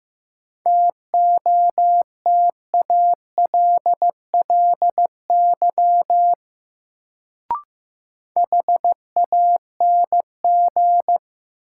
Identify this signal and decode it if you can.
Morse code, heard as a keyed tone, about 15 wpm, 700 Hz. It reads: TOTALLY HANG
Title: Sentences from Top 1600 Words - New Word Drill Alt - 15wpm